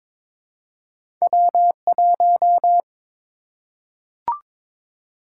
Morse code, heard as a keyed tone, about 23 wpm, 700 Hz. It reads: W1
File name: practice / 22wpm / Two Letter-Number Combinations - 22wpm